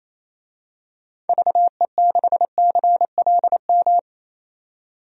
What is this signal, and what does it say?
Morse code, heard as a keyed tone, about 28 wpm, 700 Hz. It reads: VE6CLM
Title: DXCC-100 Call Signs - 6 Characters - 28wpm